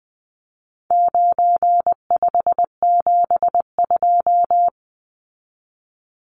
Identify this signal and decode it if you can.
Morse code, heard as a keyed tone, about 19 wpm, 700 Hz. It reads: 9572
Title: Numbers - 4-Digit 20wpm